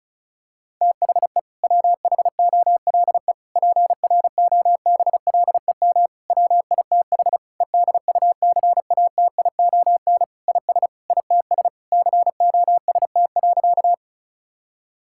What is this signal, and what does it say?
Morse code, heard as a keyed tone, about 34 wpm, 700 Hz. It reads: THE WHOLE PROBLEM WITH EDUCATION IS ITS COST.
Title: Sentences from Top 1000 Words - Review Part 6 - 35wpm